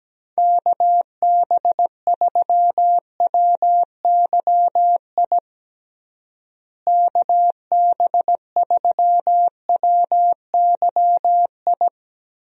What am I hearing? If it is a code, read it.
Morse code, heard as a keyed tone, about 17 wpm, 700 Hz. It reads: KB3WYI KB3WYI
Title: USA Call Signs - 6 Characters - Repeated 3x - 17wpm